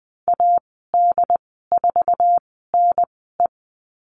Morse code, heard as a keyed tone, about 20 wpm, spaced as for 15 wpm, 700 Hz. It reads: AD4NE
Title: 161 Lesson 26g - Letter g - Call Signs 20wpm